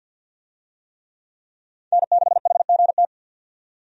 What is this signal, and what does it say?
Morse code, heard as a keyed tone, about 45 wpm, 700 Hz. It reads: N6HGT
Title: USA Call Signs - 5 Characters - 50wpm